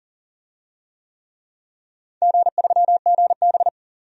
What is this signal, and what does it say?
Morse code, heard as a keyed tone, about 40 wpm, 700 Hz. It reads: G3GB
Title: DXCC-100 Call Signs - 4 Characters - 40wpm